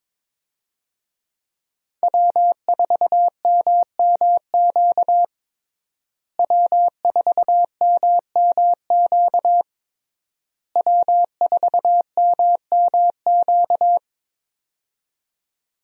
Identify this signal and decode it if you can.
Morse code, heard as a keyed tone, about 22 wpm, 700 Hz. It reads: W4MMQ W4MMQ W4MMQ
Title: USA Call Signs - 5 Characters - Repeated 3x - 22wpm